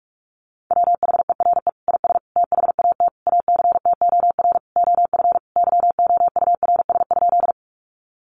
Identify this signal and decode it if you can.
Morse code, heard as a keyed tone, about 44 wpm, 700 Hz. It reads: WHERE IS THAT ACTOR OF YOURS?
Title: Sentences from Top 1500 Words 45wpm